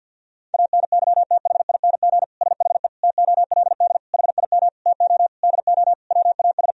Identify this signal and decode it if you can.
Morse code, heard as a keyed tone, about 50 wpm, 700 Hz. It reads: ANYTHING SHE TOLD HIM TO DO WAS
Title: Sentences from Top 1000 Words - Review Part 3 - 50wpm